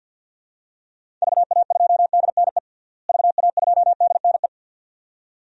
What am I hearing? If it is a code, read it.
Morse code, heard as a keyed tone, about 52 wpm, 700 Hz. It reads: VA2DNE VA2DNE
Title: DXCC-100 Call Signs - 6 Characters - Repeated 2x - 50wpm